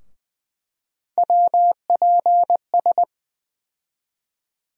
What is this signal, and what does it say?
Morse code, heard as a keyed tone, about 20 wpm, 700 Hz. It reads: WPS
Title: USA Call Sign Postfixes - 3 Letters 20wpm